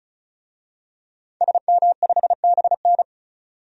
Morse code, heard as a keyed tone, about 34 wpm, 700 Hz. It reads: SM5BN
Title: DXCC-100 Call Signs - 5 Characters - 35wpm